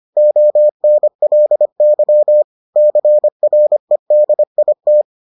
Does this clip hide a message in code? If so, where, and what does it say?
Yes: Morse code from 0.2 to 5.0 s, heard as a keyed tone, about 25 wpm, 600 Hz. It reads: ONLY CREDIT